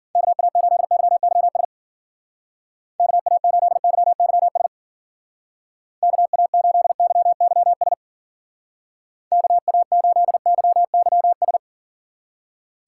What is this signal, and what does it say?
Morse code, heard as a keyed tone, about 51 wpm, 700 Hz. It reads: KA8YYS KA8YYS KA8YYS KA8YYS